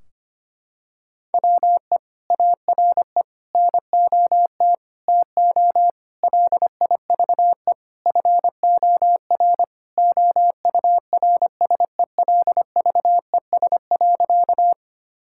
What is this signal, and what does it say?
Morse code, heard as a keyed tone, about 25 wpm, 700 Hz. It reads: WE ARE NOT TO LIVE FOR OURSELVES.
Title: Sentences from Top 1600 Words 25wpm